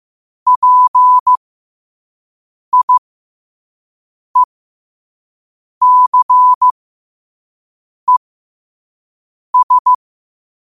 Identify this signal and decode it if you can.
Morse code, heard as a keyed tone, about 15 wpm, spaced as for 5 wpm, 1000 Hz. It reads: PIECES
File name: qotdfiles / 5wpm